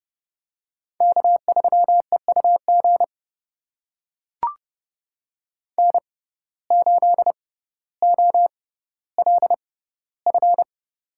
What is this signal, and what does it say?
Morse code, heard as a keyed tone, about 31 wpm, spaced as for 16 wpm, 700 Hz. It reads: K3EUG N8OLF
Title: USA Call Signs - 5 Characters - 6x Character Spacing - 30wpm